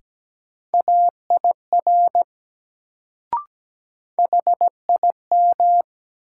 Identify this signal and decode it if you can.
Morse code, heard as a keyed tone, about 17 wpm, 700 Hz. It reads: AIR HIM
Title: Sets of 2 Words - New Word Drill - 17wpm